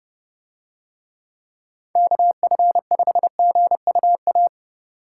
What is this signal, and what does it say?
Morse code, heard as a keyed tone, about 29 wpm, 700 Hz. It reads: KF5GUA